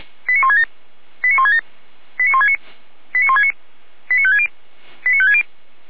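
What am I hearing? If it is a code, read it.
Five-tone selective call (ZVEI), heard as quick runs of five tones, tones of about 70 ms. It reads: 89167 89167 89169 89169 89560 89560
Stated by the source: ZVEI five-tone call 89220 received by SDR, archived by Feuerwehr Orlamünde.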